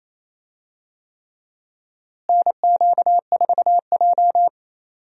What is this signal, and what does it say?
Morse code, heard as a keyed tone, about 28 wpm, 700 Hz. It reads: NQ4J
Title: USA Call Signs - 4 Characters - Repeated 3x - 28wpm